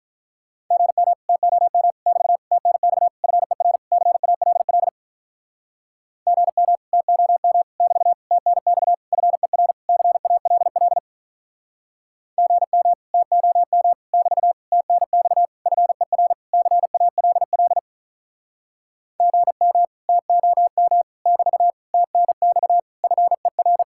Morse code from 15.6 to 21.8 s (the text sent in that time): FER CALL GM TOM <BT>